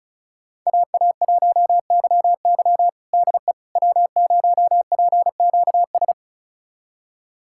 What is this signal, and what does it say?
Morse code, heard as a keyed tone, about 34 wpm, 690 Hz. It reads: AA1YY DE W0PQS
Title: USA Callsign DE Callsign - 5 Characters - 35wpm